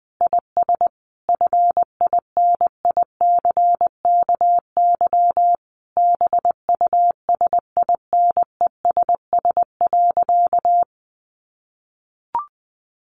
Morse code, heard as a keyed tone, about 20 wpm, 700 Hz. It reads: IS FINICKY BUSINESS.
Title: A Sound Of Thunder 20wpm